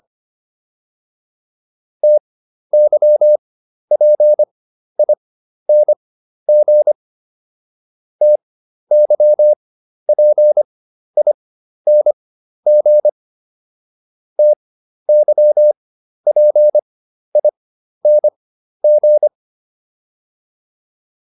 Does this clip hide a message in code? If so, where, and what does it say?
Yes: Morse code from 2.0 to 19.3 s, heard as a keyed tone, about 25 wpm, spaced as for 12 wpm, 600 Hz. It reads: TYPING TYPING TYPING